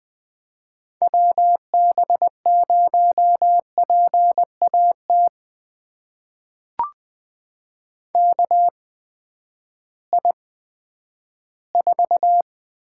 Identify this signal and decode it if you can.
Morse code, heard as a keyed tone, about 20 wpm, spaced as for 11 wpm, 700 Hz. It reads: WB0PAT KI4
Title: USA Call Signs - 6 Characters - 8x Character Spacing - 20wpm